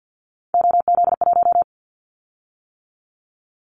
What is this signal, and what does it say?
Morse code, heard as a keyed tone, about 50 wpm, 700 Hz. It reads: OZ1
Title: Callsign Prefixes for Top 100 DXCC - 50wpm